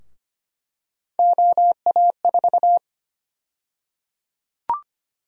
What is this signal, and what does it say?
Morse code, heard as a keyed tone, about 25 wpm, 700 Hz. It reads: OA4
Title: USA Call Sign Prefixes - Repeated 2x - 25wpm